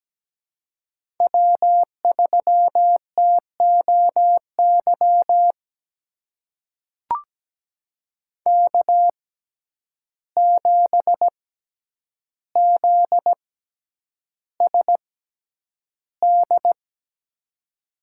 Morse code, heard as a keyed tone, about 17 wpm, spaced as for 9 wpm, 700 Hz. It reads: W3TOY K7ZSD
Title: USA Call Signs - 5 Characters - 6x Character Spacing - 17wpm